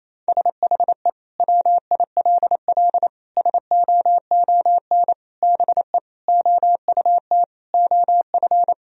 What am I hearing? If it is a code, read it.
Morse code, heard as a keyed tone, about 27 wpm, 700 Hz. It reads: SHE WILL SOON BE OUT OF
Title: Sentences from Top 1000 Words - Review Part 2 - 28wpm